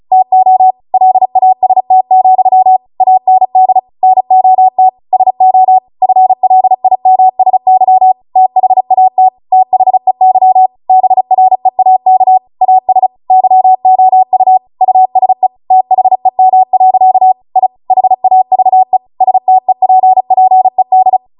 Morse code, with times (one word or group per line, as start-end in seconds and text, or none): 0.1-0.7 TO
0.9-2.8 LAST,
3.0-3.8 AND
4.0-4.9 NOT
5.1-5.8 SO
6.0-8.1 FLIMSY
8.4-9.3 THAT
9.5-10.7 THEY
10.9-12.4 BREAK
12.6-13.1 AS
13.3-14.6 YOU
14.8-15.5 USE
15.7-17.3 THEM.
17.6-17.7 I
17.9-19.0 HAVE
19.2-21.2 STEPPED